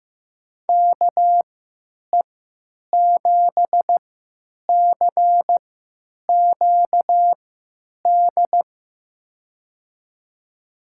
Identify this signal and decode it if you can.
Morse code, heard as a keyed tone, about 15 wpm, spaced as for 9 wpm, 700 Hz. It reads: KE7CQD
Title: USA Call Signs - 6 Characters - 3x Character Spacing - 15wpm